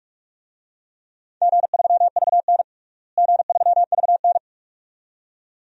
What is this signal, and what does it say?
Morse code, heard as a keyed tone, about 46 wpm, 700 Hz. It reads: G3VN G3VN
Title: DXCC-100 Call Signs - 4 Characters - Repeated 2x - 45wpm